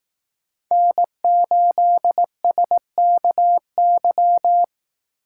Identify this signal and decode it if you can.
Morse code, heard as a keyed tone, about 18 wpm, 700 Hz. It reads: N8SKY
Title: Speed Racing - USA Call Signs 15wpm